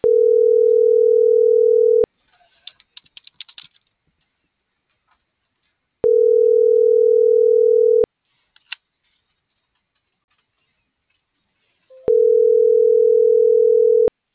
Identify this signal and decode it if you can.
Ringback tone (North America), 440 + 480 Hz.